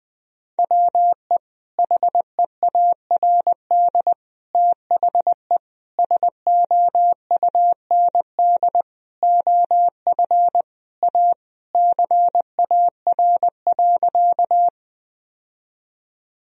Morse code, heard as a keyed tone, about 20 wpm, 700 Hz. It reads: WE HEARD THE SOUND OF A CAR.
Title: Sentences from Top 1000 Words - Review Part 1 - 20wpm